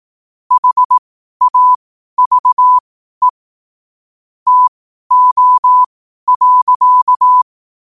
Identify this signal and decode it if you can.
Morse code, heard as a keyed tone, about 18 wpm, spaced as for 12 wpm, 1000 Hz. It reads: HAVE TO.
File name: qotdfiles / 12wpm